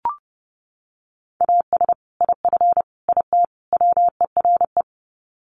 Morse code, heard as a keyed tone, about 30 wpm, 700 Hz. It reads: AS IF IT WERE
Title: Sets of 4 Words 30wpm